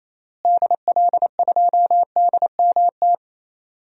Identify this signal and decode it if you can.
Morse code, heard as a keyed tone, about 28 wpm, 700 Hz. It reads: DL2DMT
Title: DXCC-100 Call Signs - 6 Characters - Repeated 2x - 28wpm